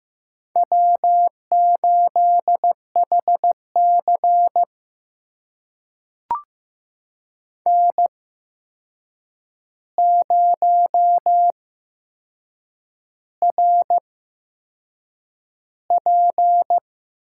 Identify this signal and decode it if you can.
Morse code, heard as a keyed tone, about 15 wpm, spaced as for 6 wpm, 700 Hz. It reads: W8HC N0RP
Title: USA Call Signs - 4 Characters - 8x Character Spacing - 15wpm